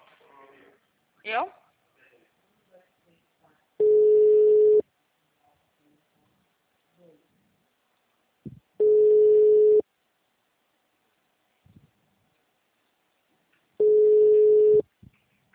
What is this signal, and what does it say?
Ringback tone (Europe, 425 Hz), with speech.